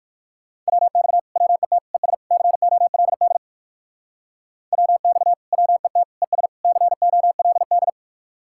Morse code, heard as a keyed tone, about 57 wpm, 700 Hz. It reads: WX WET ES COLD WX WET ES COLD